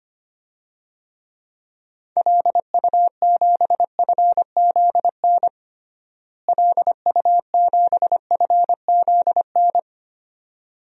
Morse code, heard as a keyed tone, about 25 wpm, 700 Hz. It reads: LU7FZN LU7FZN
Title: DXCC-100 Call Signs - 6 Characters - Repeated 2x - 25wpm